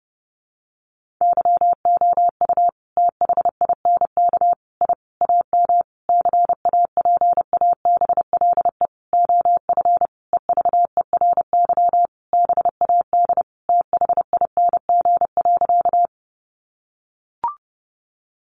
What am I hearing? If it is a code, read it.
Morse code, heard as a keyed tone, about 30 wpm, 700 Hz. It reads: YOU THINK I AM CAPABLE OF EVERY BAD THING.
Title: Sentences from Top 1200 Words 30wpm